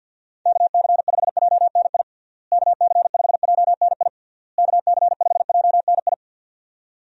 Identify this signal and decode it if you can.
Morse code, heard as a keyed tone, about 50 wpm, 700 Hz. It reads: KC5JNI KC5JNI KC5JNI
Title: USA Call Signs - 6 Characters - Repeated 3x - 50wpm